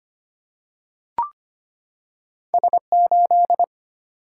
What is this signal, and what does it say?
Morse code, heard as a keyed tone, about 25 wpm, 700 Hz. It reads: S8
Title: Single-Letter Single-Number 25wpm